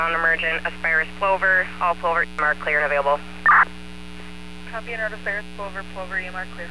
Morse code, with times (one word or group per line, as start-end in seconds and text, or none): none